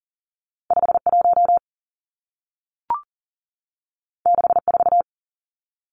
Morse code, heard as a keyed tone, about 40 wpm, 700 Hz. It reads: 51 64
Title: Two Challenging Letter-Number Combinations 40wpm